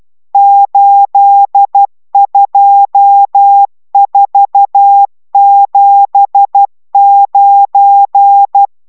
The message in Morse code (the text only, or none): 82479